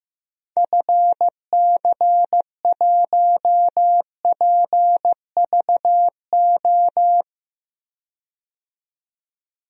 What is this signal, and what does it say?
Morse code, heard as a keyed tone, about 15 wpm, 700 Hz. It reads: FC1PVO
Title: DXCC-100 Call Signs - 6 Characters - 15wpm